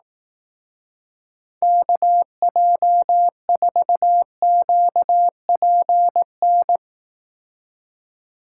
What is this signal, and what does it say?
Morse code, heard as a keyed tone, about 18 wpm, 700 Hz. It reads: KJ4QPN
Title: Speed Racing - USA 2x3 Call Signs 15wpm